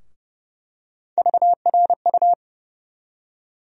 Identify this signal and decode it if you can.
Morse code, heard as a keyed tone, about 30 wpm, 700 Hz. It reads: VRU
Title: USA Call Sign Postfixes - 3 Letters 30wpm